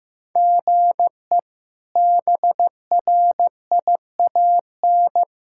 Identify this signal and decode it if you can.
Morse code, heard as a keyed tone, about 16 wpm, 700 Hz. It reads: GE BRIAN
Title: K1USN SST QSO Practice - 15wpm